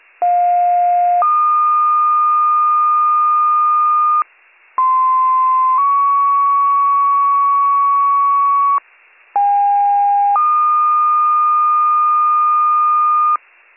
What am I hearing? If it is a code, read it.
Two-tone page, heard as a one-second tone, then a three-second tone, about 710 then 1220 Hz, about 1040 then 1120 Hz, about 810 then 1220 Hz. It is heard from 0.2 to 13.4 s.